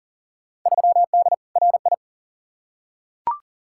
Morse code, heard as a keyed tone, about 40 wpm, 700 Hz. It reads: 3D RI
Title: Field Day Exchanges Encore - 40wpm